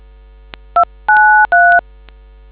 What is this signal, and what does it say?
DTMF, heard as telephone touch tones, tones of about 290 ms. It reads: 293